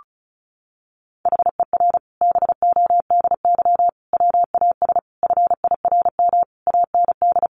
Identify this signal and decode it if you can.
Morse code, heard as a keyed tone, about 34 wpm, 700 Hz. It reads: HER BODY WAS FIRM AND
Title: Sentences from Top 1400 Words 35wpm